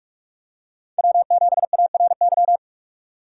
Morse code, heard as a keyed tone, about 42 wpm, 700 Hz. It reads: W7ARY
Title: USA Call Signs - 4x Character Spacing - 45wpm